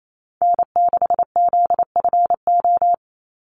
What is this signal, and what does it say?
Morse code, heard as a keyed tone, about 28 wpm, 700 Hz. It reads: N6ZFO
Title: Call Signs Repeated 2x - 28wpm